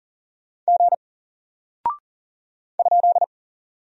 Morse code, heard as a keyed tone, about 40 wpm, 700 Hz. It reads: G ?
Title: Single Character (Letter:Number:Punctuation:Prosign) Alt 40wpm